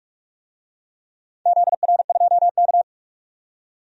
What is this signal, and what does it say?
Morse code, heard as a keyed tone, about 46 wpm, 690 Hz. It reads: ZR2K
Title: DXCC-100 Call Signs - 4 Characters - 45wpm